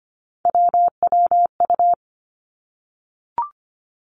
Morse code, heard as a keyed tone, about 25 wpm, 700 Hz. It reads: WWU
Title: Three Challenging Letter-Number Combinations 25wpm